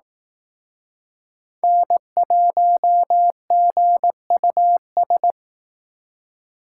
Morse code, heard as a keyed tone, about 18 wpm, 700 Hz. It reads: N1GUS
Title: Speed Racing - USA 1x3 Call Signs 15wpm